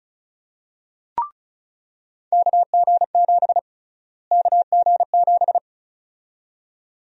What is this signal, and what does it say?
Morse code, heard as a keyed tone, about 36 wpm, 700 Hz. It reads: KG7 KG7
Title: USA Call Sign Prefixes - Repeated 2x - 35wpm